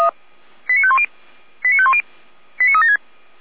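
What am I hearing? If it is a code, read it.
Five-tone selective call (ZVEI), heard as quick runs of five tones, tones of about 70 ms. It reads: 89510 89510 89376
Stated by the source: ZVEI five-tone call 89001 received by SDR.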